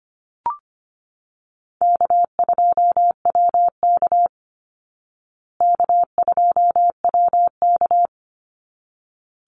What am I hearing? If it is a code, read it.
Morse code, heard as a keyed tone, about 25 wpm, 700 Hz. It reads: K2WK K2WK
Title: Call Signs Repeated 3x - 25wpm